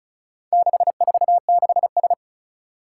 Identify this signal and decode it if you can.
Morse code, heard as a keyed tone, about 36 wpm, 700 Hz. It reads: 646S
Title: Four Challenging Letter-Number Combinations 35wpm